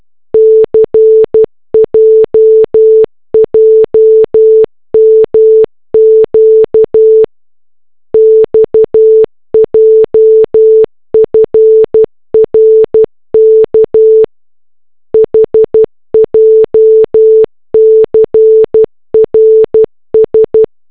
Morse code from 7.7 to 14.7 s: XJFRK